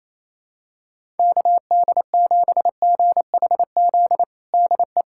Morse code, heard as a keyed tone, about 28 wpm, 700 Hz. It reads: KD7GHZ DE
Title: USA Callsign DE Callsign - 6 Characters - 28wpm